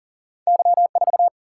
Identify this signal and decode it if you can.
Morse code, heard as a keyed tone, about 39 wpm, 700 Hz. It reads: Y4
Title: Single-Letter Single-Number 40wpm